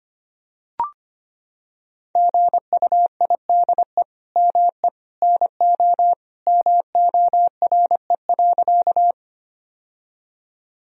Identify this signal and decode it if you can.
Morse code, heard as a keyed tone, about 25 wpm, 700 Hz. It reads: GUIDE ME NO MORE.
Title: Sentences from Top 1000 Words - Review Part 3 - 25wpm